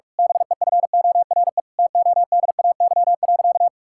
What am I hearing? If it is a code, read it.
Morse code, heard as a keyed tone, about 41 wpm, 700 Hz. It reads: BEFORE TODAY.
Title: Sentences from Top 1000 Words - Review Part 6 - 45wpm